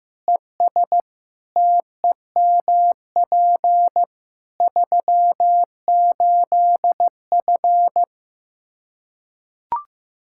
Morse code, heard as a keyed tone, about 15 wpm, 700 Hz. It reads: ES TEMP 38F
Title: Practice QSO - Exchange 2 - Info - 15wpm